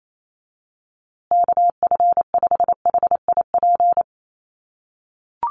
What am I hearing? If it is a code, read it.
Morse code, heard as a keyed tone, about 28 wpm, 700 Hz. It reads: KF5HIP
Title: USA Call Signs Repeated 3x 28wpm